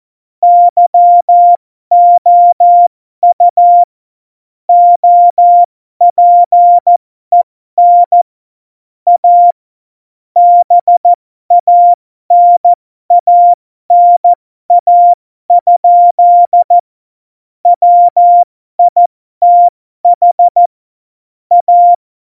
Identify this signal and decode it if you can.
Morse code, heard as a keyed tone, about 14 wpm, spaced as for 12 wpm, 700 Hz. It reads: YOU OPEN A BANANA? WITH A
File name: Morsekurs / Witze